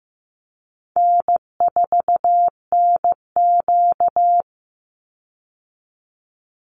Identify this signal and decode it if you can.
Morse code, heard as a keyed tone, about 15 wpm, 700 Hz. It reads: N4NQ
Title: Call Signs Repeated 3x - 15wpm